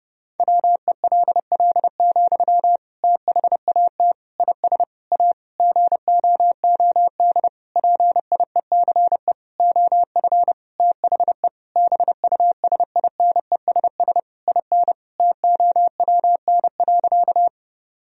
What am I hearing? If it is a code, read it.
Morse code, heard as a keyed tone, about 30 wpm, 700 Hz. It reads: WELL, THAT IS A GOOD PIECE OF THE BUSINESS IN TOWN.
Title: Sentences from Top 1000 Words - Review Part 2 - 30wpm